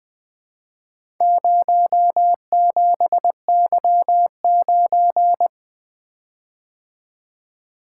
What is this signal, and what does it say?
Morse code, heard as a keyed tone, about 20 wpm, 700 Hz. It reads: 07Y9
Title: Four Letter-Number Combinations - 20wpm